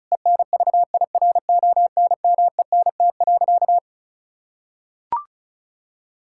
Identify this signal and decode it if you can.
Morse code, heard as a keyed tone, about 36 wpm, 700 Hz. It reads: ENVIRONMENT.
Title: Sentences from Top 1000 Words - Review Part 6 - 35wpm